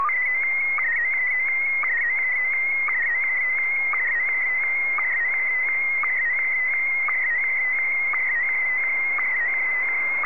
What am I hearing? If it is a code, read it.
SSTV (slow-scan television), Scottie DX.